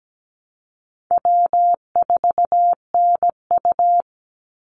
Morse code, heard as a keyed tone, about 17 wpm, 700 Hz. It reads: W4NU
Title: Easy Call Signs Repeated 3x - 17wpm